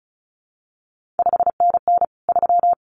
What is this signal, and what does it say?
Morse code, heard as a keyed tone, about 36 wpm, 700 Hz. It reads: 5NN 3
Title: CQ WW DX Contest Exchange - 35wpm